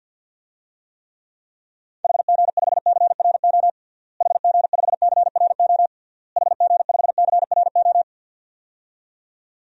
Morse code, heard as a keyed tone, about 50 wpm, 700 Hz. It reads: HG5CRO HG5CRO HG5CRO
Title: DXCC-100 Call Signs - 6 Characters - Repeated 3x - 50wpm